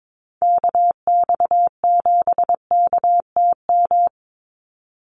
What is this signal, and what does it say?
Morse code, heard as a keyed tone, about 22 wpm, 700 Hz. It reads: KX7KTM
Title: USA Call Signs 22wpm